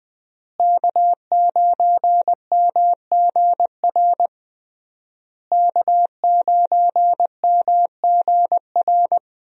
Morse code, heard as a keyed tone, about 20 wpm, 700 Hz. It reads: K9MGR K9MGR
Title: USA Call Signs - 5 Characters - Repeated 3x - 20wpm